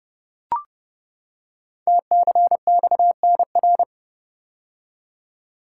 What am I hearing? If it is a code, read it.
Morse code, heard as a keyed tone, about 30 wpm, 700 Hz. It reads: TCXNR
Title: Code Groups - 5 Letters - 30wpm